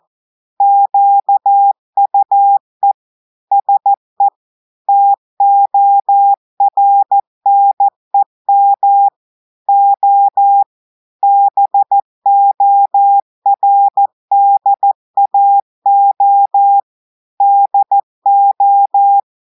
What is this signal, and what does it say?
Morse code, heard as a keyed tone, about 14 wpm, 800 Hz. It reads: QUE SE TORNEM O BORDAO DO
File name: audios / portugais / portugais texte en clair 2 vitesse 14 mots minutes